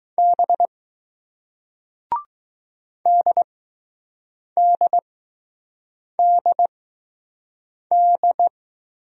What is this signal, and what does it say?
Morse code, heard as a keyed tone, about 20 wpm, spaced as for 6 wpm, 700 Hz. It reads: B DDDD